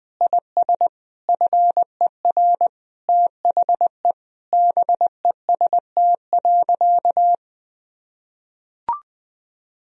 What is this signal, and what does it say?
Morse code, heard as a keyed tone, about 20 wpm, 700 Hz. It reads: IS FER THE BEST.
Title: Sentences from Top 500 Words and Abbreviations 20wpm